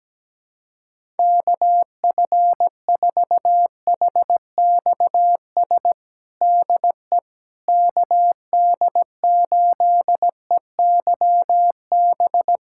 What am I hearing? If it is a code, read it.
Morse code, heard as a keyed tone, about 17 wpm, 700 Hz. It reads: KF4HXS DE KD8EYB
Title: USA Callsign DE Callsign - 6 Characters - 17wpm